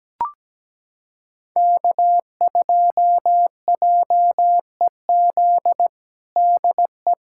Morse code, heard as a keyed tone, about 17 wpm, 700 Hz. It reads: K2JEZ DE
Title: USA Callsign DE Callsign - 5 Characters - 17wpm